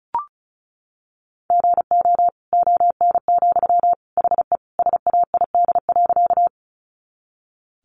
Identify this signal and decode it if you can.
Morse code, heard as a keyed tone, about 34 wpm, 700 Hz. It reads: GO ON, HE SAID.
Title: Sentences from Top 100 Words 35wpm